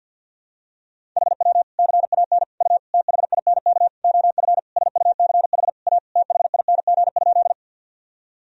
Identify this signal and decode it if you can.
Morse code, heard as a keyed tone, about 50 wpm, 700 Hz. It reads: HW CAN U THINK OF SUCH A THING?
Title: Sentences from Top 200 Words and Abbreviations 50wpm